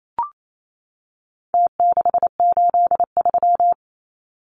Morse code, heard as a keyed tone, about 28 wpm, 700 Hz. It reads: T683
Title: Numbers - Contest Style - 4-Digit 28wpm